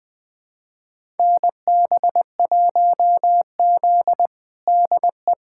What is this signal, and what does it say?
Morse code, heard as a keyed tone, about 21 wpm, 700 Hz. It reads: NB1Z DE
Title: USA Callsign DE Callsign - 4 Characters - 20wpm